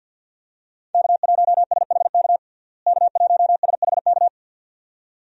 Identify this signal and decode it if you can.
Morse code, heard as a keyed tone, about 46 wpm, 700 Hz. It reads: K1SHK K1SHK
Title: USA Call Signs - 5 Characters - Repeated 2x - 50wpm